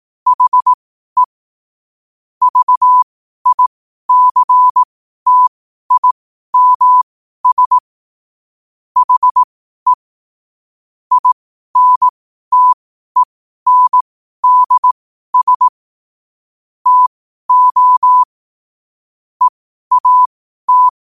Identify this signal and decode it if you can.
Morse code, heard as a keyed tone, about 18 wpm, spaced as for 12 wpm, 1000 Hz. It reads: HE VICTIMS HE INTENDS TO EAT
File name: qotdfiles / 12wpm